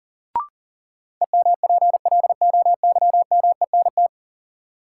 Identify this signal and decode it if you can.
Morse code, heard as a keyed tone, about 40 wpm, 700 Hz. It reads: EMPLOYMENT
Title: Sentences from Top 1500 Words - New Word Drill - 40wpm